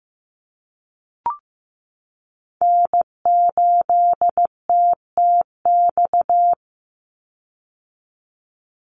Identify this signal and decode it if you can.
Morse code, heard as a keyed tone, about 15 wpm, 700 Hz. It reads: N8TTX
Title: USA Call Signs 15wpm